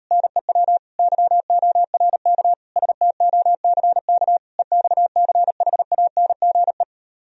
Morse code, heard as a keyed tone, about 39 wpm, 700 Hz. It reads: NEW YORK STOCK EXCHANGE